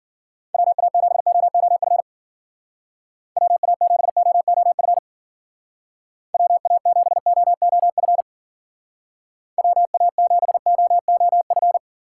Morse code, heard as a keyed tone, about 50 wpm, 700 Hz. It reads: WA7OOF WA7OOF WA7OOF WA7OOF
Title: Speed Racing - USA Call Signs 40wpm